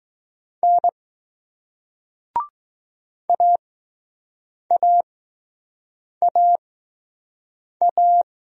Morse code, heard as a keyed tone, about 20 wpm, spaced as for 6 wpm, 700 Hz. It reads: N AAAA